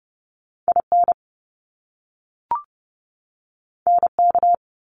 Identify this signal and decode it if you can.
Morse code, heard as a keyed tone, about 30 wpm, 700 Hz. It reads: IN NK
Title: Top Two Letter Combinations in Words 30wpm